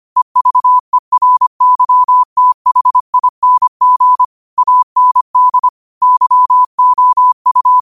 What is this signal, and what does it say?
Morse code, heard as a keyed tone, about 25 wpm, 1000 Hz. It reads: EVERYTHING AND YOU
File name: qotdfiles / 25wpm